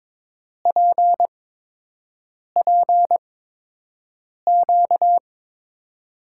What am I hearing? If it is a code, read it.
Morse code, heard as a keyed tone, about 22 wpm, spaced as for 6 wpm, 700 Hz. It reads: PPQ
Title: USA Call Signs - 6 Characters - 8x Character Spacing - 22wpm